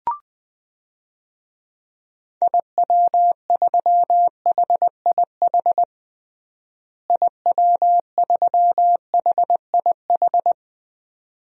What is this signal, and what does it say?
Morse code, heard as a keyed tone, about 20 wpm, 700 Hz. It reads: IW3HIH IW3HIH
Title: DXCC-100 Call Signs - 6 Characters - Repeated 2x - 20wpm